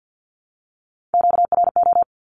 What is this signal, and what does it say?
Morse code, heard as a keyed tone, about 48 wpm, 700 Hz. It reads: QRO
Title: Q-Codes 50wpm